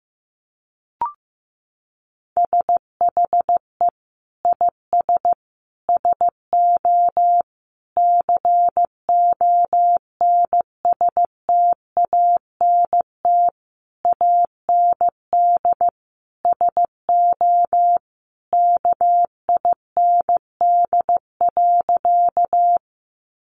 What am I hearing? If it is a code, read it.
Morse code, heard as a keyed tone, about 15 wpm, 700 Hz. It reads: SHE IS SO CONSTANT AND SO KIND.
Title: Sentences from Top 1500 Words 15wpm